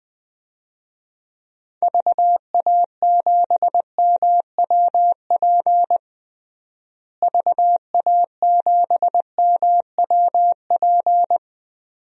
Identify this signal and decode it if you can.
Morse code, heard as a keyed tone, about 20 wpm, 700 Hz. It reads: VA7MWP VA7MWP
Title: DXCC-100 Call Signs - 6 Characters - Repeated 2x - 20wpm